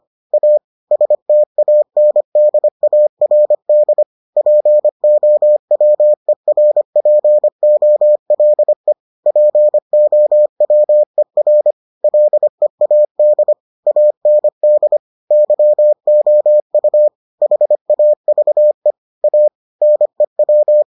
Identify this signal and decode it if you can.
Morse code, heard as a keyed tone, about 25 wpm, 600 Hz. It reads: A STANDARD POWERPOLE POWER LEAD AND YOU HAVE A NEW